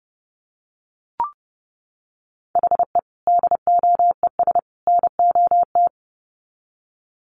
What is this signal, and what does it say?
Morse code, heard as a keyed tone, about 30 wpm, 700 Hz. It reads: HE DOES NOT
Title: Sets of 3 Words 30wpm